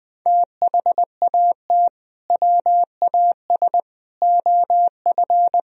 Morse code, heard as a keyed tone, about 20 wpm, 700 Hz. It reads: THAT WAS OF
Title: Sentences from Top 1000 Words - Review Part 2 - 20wpm